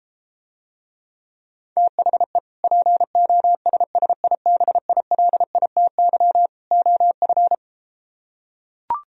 Morse code, heard as a keyed tone, about 34 wpm, 700 Hz. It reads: THE POSSIBILITY OF